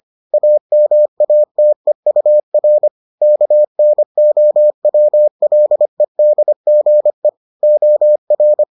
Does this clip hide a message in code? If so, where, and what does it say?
Yes: Morse code from 0.3 to 8.6 s, heard as a keyed tone, about 25 wpm, 600 Hz. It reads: AMATEUR KNOWLEDGE OR